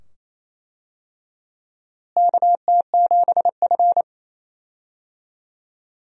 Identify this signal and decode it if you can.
Morse code, heard as a keyed tone, about 28 wpm, 700 Hz. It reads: KT7F
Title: USA Call Signs - 4 Characters - Repeated 2x - 28wpm